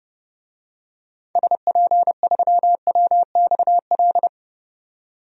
Morse code, heard as a keyed tone, about 32 wpm, 700 Hz. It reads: SP3WXL
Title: DXCC-100 Call Signs - 6 Characters - 30wpm